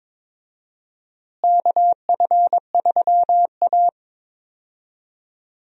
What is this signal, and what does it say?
Morse code, heard as a keyed tone, about 22 wpm, 700 Hz. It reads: KF3A
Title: USA Call Signs - 4 Characters - Repeated 2x - 22wpm